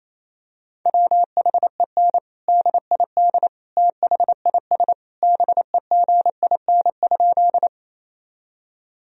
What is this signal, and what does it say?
Morse code, heard as a keyed tone, about 28 wpm, 700 Hz. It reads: WHEN DID THIS BEGIN?
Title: Sentences from Top 1000 Words - Review Part 1 - 28wpm